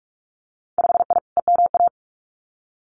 Morse code, heard as a keyed tone, about 46 wpm, 700 Hz. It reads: 5I EMA
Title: Winter Field Day Exchanges Encore 45wpm